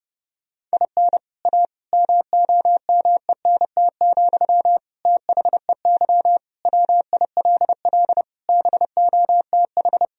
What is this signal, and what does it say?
Morse code, heard as a keyed tone, about 30 wpm, 700 Hz. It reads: IN A MOMENT, THEY WILL BOTH
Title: Sentences from Top 1000 Words - Review Part 2 - 30wpm